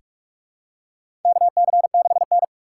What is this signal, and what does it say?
Morse code, heard as a keyed tone, about 47 wpm, 700 Hz. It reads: KC6N
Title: USA Call Signs - 4 Characters - 2x Character Spacing - 45wpm